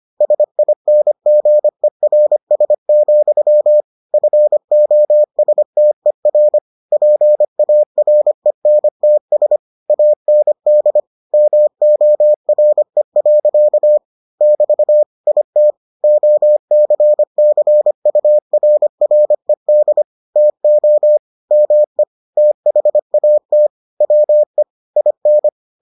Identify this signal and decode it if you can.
Morse code, heard as a keyed tone, about 25 wpm, 600 Hz. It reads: SINGERS, FOSTER PARENTS AND MORE. <BT> IT OCCURRED TO ME THAT WE IN